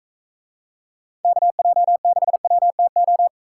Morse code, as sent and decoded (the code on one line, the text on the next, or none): -.- .--- -.... .-- - ---
KJ6WTO